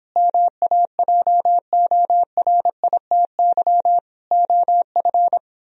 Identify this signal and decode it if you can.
Morse code, heard as a keyed tone, about 29 wpm, 700 Hz. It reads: MAJORITY OF